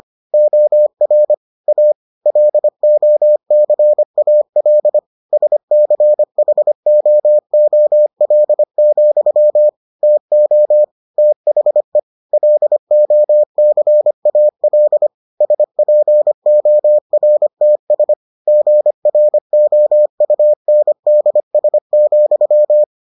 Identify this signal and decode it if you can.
Morse code, heard as a keyed tone, about 25 wpm, 600 Hz. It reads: OR A LOCAL SCHOOL, TO THE LOCAL SPORTS GROUNDS,